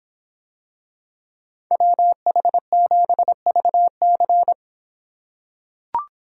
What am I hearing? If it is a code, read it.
Morse code, heard as a keyed tone, about 26 wpm, 700 Hz. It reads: WH7VC